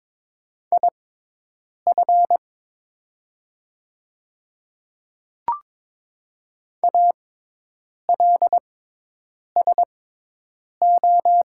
Morse code, heard as a keyed tone, about 22 wpm, spaced as for 8 wpm, 700 Hz. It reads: IF ALSO
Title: 6x Character Spacing - Top 200 Words - 22wpm